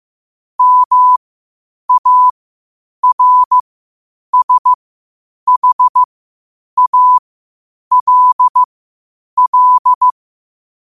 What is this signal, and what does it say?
Morse code, heard as a keyed tone, about 15 wpm, spaced as for 8 wpm, 1000 Hz. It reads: MARSHALL
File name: qotdfiles / 8wpm